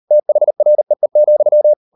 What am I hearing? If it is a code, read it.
Morse code, heard as a keyed tone, about 38 wpm, 600 Hz. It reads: THREE,